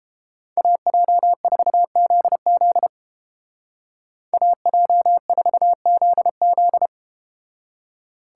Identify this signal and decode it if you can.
Morse code, heard as a keyed tone, about 29 wpm, 700 Hz. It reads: AJ4ZZ AJ4ZZ